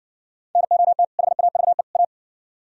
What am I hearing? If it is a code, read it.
Morse code, heard as a keyed tone, about 60 wpm, 700 Hz. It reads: NOT HAVE A